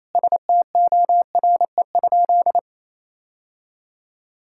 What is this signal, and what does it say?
Morse code, heard as a keyed tone, about 28 wpm, 700 Hz. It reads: STORE?
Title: Sentences from Top 1000 Words - Review Part 5 - 28wpm